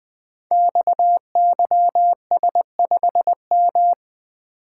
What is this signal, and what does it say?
Morse code, heard as a keyed tone, about 20 wpm, 700 Hz. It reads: XYS5M
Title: Code Groups - 5 Letter:Numbers - 20wpm